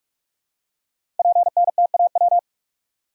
Morse code, heard as a keyed tone, about 44 wpm, 690 Hz. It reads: PNTAW
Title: Code Groups - 5 Letters - 45wpm